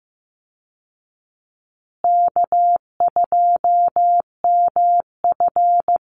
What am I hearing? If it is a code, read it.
Morse code, heard as a keyed tone, about 14 wpm, 700 Hz. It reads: K2MF
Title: Easy Call Signs Repeated 2x - 15wpm